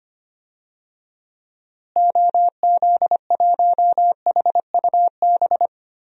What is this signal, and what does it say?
Morse code, heard as a keyed tone, about 25 wpm, 700 Hz. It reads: OZ1HUB
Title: DXCC-100 Call Signs - 6 Characters - Repeated 2x - 25wpm